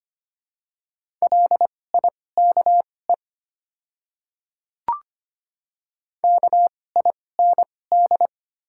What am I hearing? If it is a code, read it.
Morse code, heard as a keyed tone, about 25 wpm, spaced as for 18 wpm, 700 Hz. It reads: LIKE KIND
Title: 2x Character Spacing - Top 200 Words - 25wpm